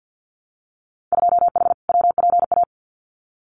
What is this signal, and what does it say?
Morse code, heard as a keyed tone, about 50 wpm, 700 Hz. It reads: 2H WPA
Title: Winter Field Day Exchanges 50wpm